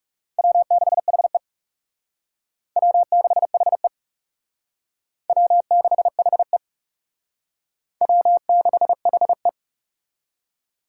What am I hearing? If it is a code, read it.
Morse code, heard as a keyed tone, about 37 wpm, 700 Hz. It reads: W6HE W6HE W6HE W6HE
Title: Speed Racing - USA 1x2 Call Signs 30wpm